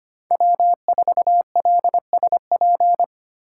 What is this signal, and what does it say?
Morse code, heard as a keyed tone, about 25 wpm, 700 Hz. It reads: W4LSP
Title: USA Call Signs - 5 Characters - 4x Character Spacing - 25wpm